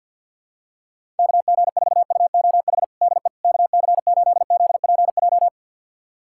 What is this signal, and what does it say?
Morse code, heard as a keyed tone, about 51 wpm, 700 Hz. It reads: KG4UOH DE KC8ZPJ
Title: USA Callsign DE Callsign - 6 Characters - 50wpm